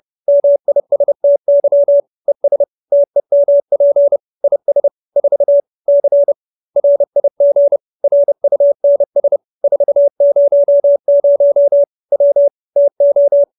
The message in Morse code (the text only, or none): MISTY ES TEMP IS 4 C RIG RUNS 400 W TO